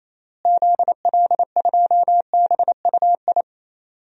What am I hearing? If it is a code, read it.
Morse code, heard as a keyed tone, about 27 wpm, 700 Hz. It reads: ZL2BUI
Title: DXCC-100 Call Signs - 6 Characters - Repeated 2x - 28wpm